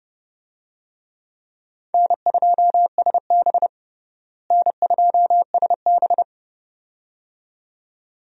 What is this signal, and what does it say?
Morse code, heard as a keyed tone, about 31 wpm, 700 Hz. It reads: N2SB N2SB
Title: USA Call Signs - 4 Characters - Repeated 2x - 30wpm